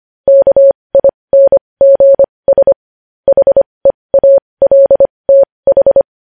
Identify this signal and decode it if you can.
Morse code, heard as a keyed tone, about 25 wpm, spaced as for 20 wpm, 550 Hz. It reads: KINGS HEALTH
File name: podcast / News headlines-1x-25x20